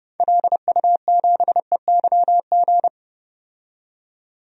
Morse code, heard as a keyed tone, about 30 wpm, 700 Hz. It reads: LU7EYG
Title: DXCC-100 Call Signs - 6 Characters - 30wpm